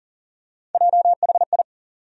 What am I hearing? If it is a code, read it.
Morse code, heard as a keyed tone, about 40 wpm, 700 Hz. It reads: JHI
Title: USA Call Sign Postfixes - 3 Letters 40wpm